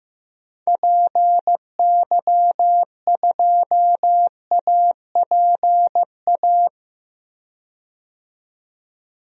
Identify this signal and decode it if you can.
Morse code, heard as a keyed tone, about 15 wpm, 700 Hz. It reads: PY2APA
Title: DXCC-100 Call Signs - 6 Characters - 15wpm